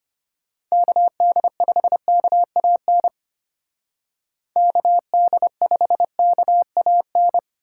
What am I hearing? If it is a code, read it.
Morse code, heard as a keyed tone, about 33 wpm, 700 Hz. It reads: KD5KAN KD5KAN